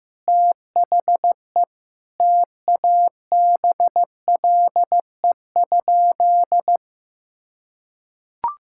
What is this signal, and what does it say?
Morse code, heard as a keyed tone, about 15 wpm, 700 Hz. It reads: THE TABLE?
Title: Sentences from Top 1600 Words 15wpm